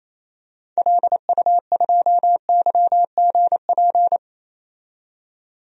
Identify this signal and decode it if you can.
Morse code, heard as a keyed tone, about 28 wpm, 700 Hz. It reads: LU2YGP
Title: DXCC-100 Call Signs - 6 Characters - 28wpm